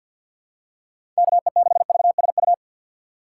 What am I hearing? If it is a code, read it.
Morse code, heard as a keyed tone, about 47 wpm, 700 Hz. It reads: KE6VSU
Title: USA Call Signs - 6 Characters - 3x Character Spacing - 50wpm